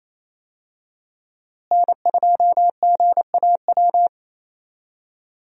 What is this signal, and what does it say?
Morse code, heard as a keyed tone, about 27 wpm, 700 Hz. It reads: N2GAW
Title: USA Call Signs - 5 Characters - Repeated 3x - 28wpm